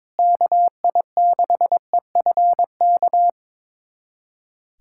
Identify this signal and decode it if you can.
Morse code, heard as a keyed tone, about 22 wpm, 700 Hz. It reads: KI6EFK
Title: USA Callsign DE Callsign - 6 Characters - 22wpm